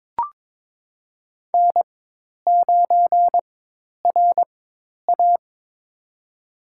Morse code, heard as a keyed tone, about 21 wpm, spaced as for 10 wpm, 700 Hz. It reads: N9RA
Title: USA Call Signs - 4 Characters - 4x Character Spacing - 22wpm